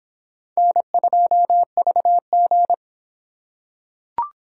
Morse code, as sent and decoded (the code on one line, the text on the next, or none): -. ..--- ...- --.
N2VG